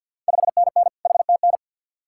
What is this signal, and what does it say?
Morse code, heard as a keyed tone, about 53 wpm, 690 Hz. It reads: 5NN HTN
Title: Ontario QSO Party Exchange 50wpm